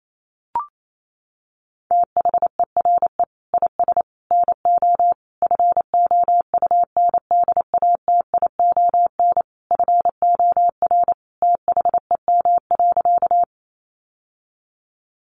Morse code, heard as a keyed tone, about 28 wpm, 700 Hz. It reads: THERE IS NO FOUNDATION FOR THEM.
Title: Sentences from Top 1200 Words 28wpm